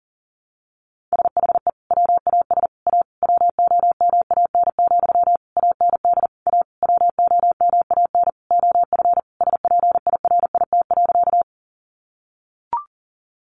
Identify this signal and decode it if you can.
Morse code, heard as a keyed tone, about 40 wpm, 700 Hz. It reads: SHE WAS A WOMAN, AND A WOMAN OF SPIRIT.
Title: Sentences from Top 1400 Words 40wpm